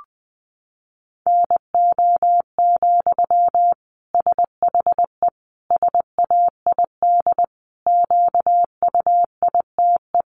Morse code, heard as a keyed tone, about 20 wpm, 700 Hz. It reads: NO, SHE SAID QUITE
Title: Sentences from Top 800 Words 20wpm